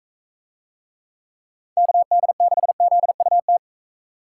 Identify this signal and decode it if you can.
Morse code, heard as a keyed tone, about 40 wpm, 700 Hz. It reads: KD6ZUT